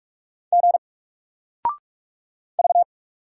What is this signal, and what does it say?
Morse code, heard as a keyed tone, about 44 wpm, 700 Hz. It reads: G V
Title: Single Character (Letter:Number:Punctuation:Prosign) 45wpm